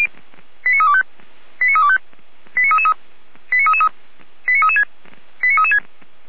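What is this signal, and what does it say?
Five-tone selective call (ZVEI), heard as quick runs of five tones, tones of about 70 ms. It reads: 89325 89325 89333 89333 89336 89336
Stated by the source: ZVEI five-tone call received on an SDR.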